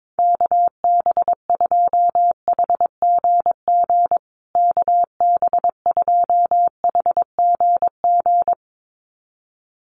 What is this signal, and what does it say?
Morse code, heard as a keyed tone, about 22 wpm, 700 Hz. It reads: KB2HGG KB2HGG
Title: USA Call Signs Repeated 2x 22wpm